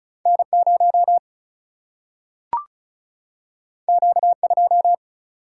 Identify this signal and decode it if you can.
Morse code, heard as a keyed tone, about 36 wpm, 700 Hz. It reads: N0 Q2
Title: Single-Letter Single-Number 35wpm